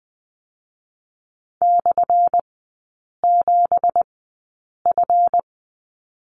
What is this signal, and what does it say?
Morse code, heard as a keyed tone, about 20 wpm, 700 Hz. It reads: / 7 F
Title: 178 Lesson 29e - Slash - Three Characters 20wpm